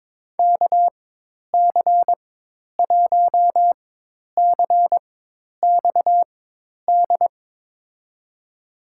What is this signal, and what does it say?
Morse code, heard as a keyed tone, about 22 wpm, spaced as for 10 wpm, 700 Hz. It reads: KC1CXD
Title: USA Call Signs - 6 Characters - 4x Character Spacing - 22wpm